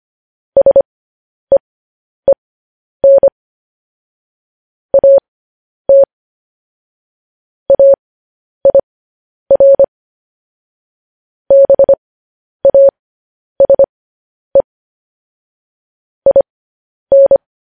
Morse code, heard as a keyed tone, about 25 wpm, spaced as for 10 wpm, 550 Hz. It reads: SEEN AT AIR BASE IN